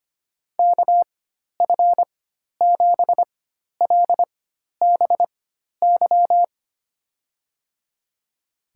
Morse code, heard as a keyed tone, about 25 wpm, spaced as for 12 wpm, 700 Hz. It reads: KF7LBY
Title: USA Call Signs - 6 Characters - 4x Character Spacing - 25wpm